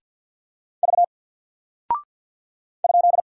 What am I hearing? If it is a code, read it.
Morse code, heard as a keyed tone, about 48 wpm, 700 Hz. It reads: V ?
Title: Single Character (Letter:Number:Punctuation:Prosign) Alt 50wpm